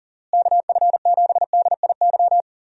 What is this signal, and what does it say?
Morse code, heard as a keyed tone, about 40 wpm, 700 Hz. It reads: KF7DIY
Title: USA Call Signs - 6 Characters - Repeated 2x - 40wpm